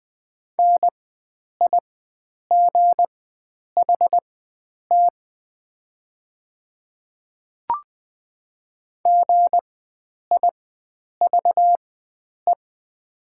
Morse code, heard as a keyed tone, about 20 wpm, spaced as for 9 wpm, 700 Hz. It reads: NIGHT GIVE
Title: 4x Character Spacing - Top 300 Words - 20wpm